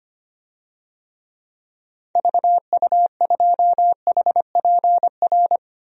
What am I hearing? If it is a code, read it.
Morse code, heard as a keyed tone, about 25 wpm, 700 Hz. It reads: VU2HPR
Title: DXCC-100 Call Signs - 6 Characters - Repeated 2x - 25wpm